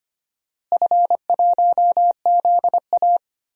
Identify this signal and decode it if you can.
Morse code, heard as a keyed tone, about 25 wpm, 700 Hz. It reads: F1ZA
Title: DXCC-100 Call Signs - 4 Characters - Repeated 3x - 25wpm